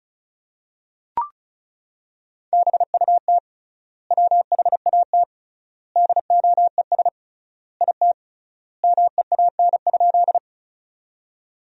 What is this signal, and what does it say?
Morse code, heard as a keyed tone, about 34 wpm, 700 Hz. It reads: BUT WHAT DOES IT MEAN?
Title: 3x Word-Spacing - Sentences from Top 200 Words - 35wpm